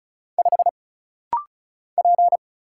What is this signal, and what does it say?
Morse code, heard as a keyed tone, about 34 wpm, 700 Hz. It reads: 5 P
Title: Single Letter-Number - Rapid-Fire 35wpm